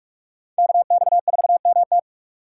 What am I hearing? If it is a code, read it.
Morse code, heard as a keyed tone, about 42 wpm, 700 Hz. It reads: KX4MT